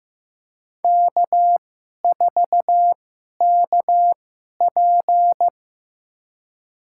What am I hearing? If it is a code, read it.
Morse code, heard as a keyed tone, about 15 wpm, spaced as for 11 wpm, 700 Hz. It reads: K4KP
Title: USA Call Signs - 2x Character Spacing - 15wpm